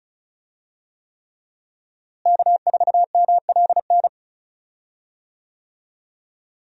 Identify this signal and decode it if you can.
Morse code, heard as a keyed tone, about 34 wpm, 700 Hz. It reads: K4MLN
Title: USA Call Signs - 5 Characters - Repeated 3x - 35wpm